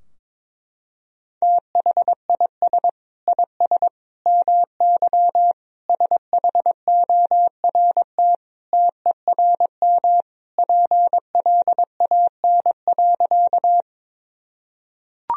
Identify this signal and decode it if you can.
Morse code, heard as a keyed tone, about 22 wpm, 700 Hz. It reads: THIS IS MY SHORT TERM PLAN.
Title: Sentences from Top 1000 Words - Review Part 4 - 22wpm